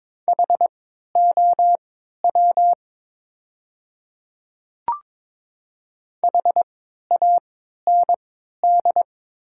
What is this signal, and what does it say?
Morse code, heard as a keyed tone, about 22 wpm, spaced as for 12 wpm, 690 Hz. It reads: HOW HAND
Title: 3x Character Spacing - Top 200 Words - 22wpm